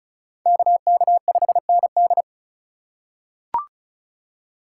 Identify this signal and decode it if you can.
Morse code, heard as a keyed tone, about 36 wpm, 700 Hz. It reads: KK5ND
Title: USA Call Signs - 5 Characters - 2x Character Spacing - 35wpm